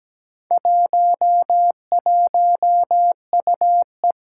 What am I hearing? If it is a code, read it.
Morse code, heard as a keyed tone, about 17 wpm, 700 Hz. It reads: 11UE
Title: Four Challenging Letter-Number Combinations 17wpm